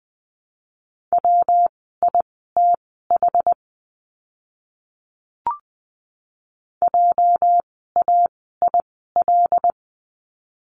Morse code, heard as a keyed tone, about 20 wpm, spaced as for 14 wpm, 700 Hz. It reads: WITH JAIL
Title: 206 Lesson 33f - Letter j - Words 20wpm